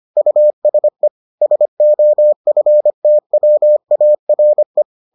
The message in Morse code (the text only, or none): USE SOFTWARE